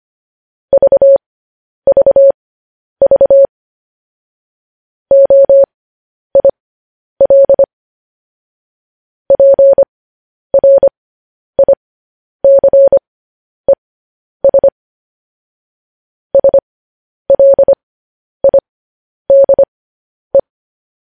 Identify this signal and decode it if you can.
Morse code, heard as a keyed tone, about 25 wpm, spaced as for 10 wpm, 550 Hz. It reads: VVV OIL PRICES SLIDE
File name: podcast / News headlines-1x-25x10